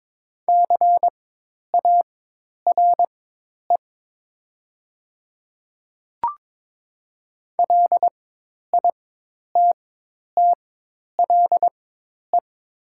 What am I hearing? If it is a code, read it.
Morse code, heard as a keyed tone, about 22 wpm, spaced as for 10 wpm, 700 Hz. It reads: CARE LITTLE
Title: 4x Character Spacing - Top 300 Words - 22wpm